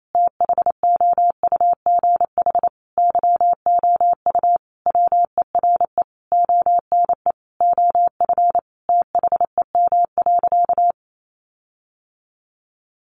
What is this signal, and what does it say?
Morse code, heard as a keyed tone, about 28 wpm, 700 Hz. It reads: THOUGH YOU WERE ONE OF THEM.
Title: Sentences from Top 1500 Words 28wpm